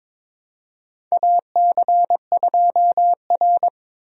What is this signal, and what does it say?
Morse code, heard as a keyed tone, about 22 wpm, 700 Hz. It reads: AC2R
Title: USA Call Signs - 4 Characters - Repeated 2x - 22wpm